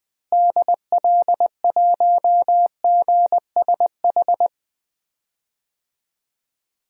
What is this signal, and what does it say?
Morse code, heard as a keyed tone, about 20 wpm, 700 Hz. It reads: DL1GSH
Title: DXCC-100 Call Signs - 6 Characters - Repeated 2x - 20wpm